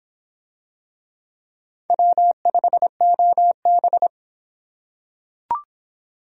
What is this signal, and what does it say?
Morse code, heard as a keyed tone, about 26 wpm, 700 Hz. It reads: W5OB